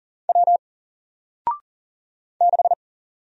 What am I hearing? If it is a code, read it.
Morse code, heard as a keyed tone, about 40 wpm, 700 Hz. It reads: W 6
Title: Single Character (Letter:Number:Punctuation:Prosign) Alt 40wpm